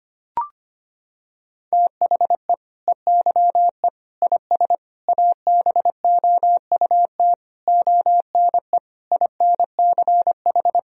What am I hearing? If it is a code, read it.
Morse code, heard as a keyed tone, about 25 wpm, 700 Hz. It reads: THE EYE IS ABOUT ONE INCH
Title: Sentences from Top 1000 Words - Review Part 3 - 25wpm